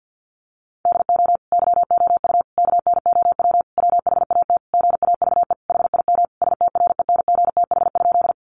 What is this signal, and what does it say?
Morse code, heard as a keyed tone, about 50 wpm, 700 Hz. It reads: DO YOU KNOW WHAT GAVE HIM STRENGTH?
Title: Sentences from Top 1300 Words 50wpm